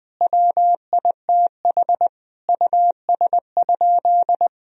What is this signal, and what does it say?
Morse code, heard as a keyed tone, about 21 wpm, 700 Hz. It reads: WITH US?
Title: Sentences from Top 1000 Words - Review Part 1 - 20wpm